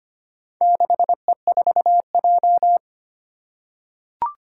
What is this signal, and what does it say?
Morse code, heard as a keyed tone, about 25 wpm, 700 Hz. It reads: 6E4J
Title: Four Challenging Letter-Number Combinations 25wpm